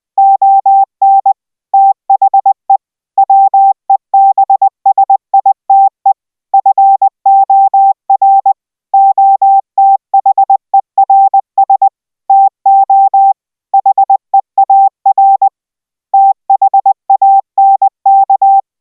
Morse code, heard as a keyed tone, about 20 wpm, 780 Hz. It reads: ON THE WEBSITE FOR OTHERS TO HEAR THANK